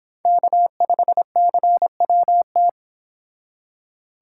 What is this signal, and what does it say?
Morse code, heard as a keyed tone, about 26 wpm, 700 Hz. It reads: K5CWT